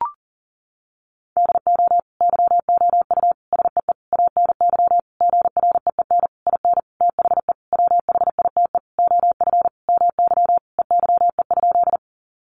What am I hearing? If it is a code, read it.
Morse code, heard as a keyed tone, about 40 wpm, 700 Hz. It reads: DO YOU SEE ANY GREEN IN THE WHITE OF MY EYE?
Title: Sentences from Top 500 Words 40wpm